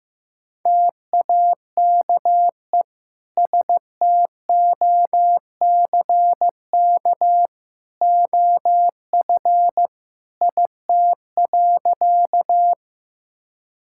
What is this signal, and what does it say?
Morse code, heard as a keyed tone, about 15 wpm, 700 Hz. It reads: TAKE STOCK OF IT.
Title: Sentences from Top 1000 Words - Review Part 5 - 15wpm